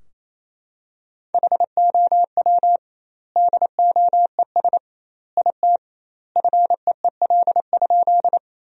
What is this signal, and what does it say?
Morse code, heard as a keyed tone, about 28 wpm, 700 Hz. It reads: HOW DOES IT FEEL?
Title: 2x Word-Spacing - Sentences from Top 400 Words - 28wpm